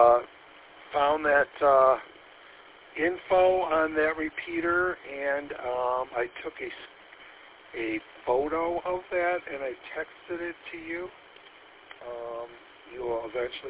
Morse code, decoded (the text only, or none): none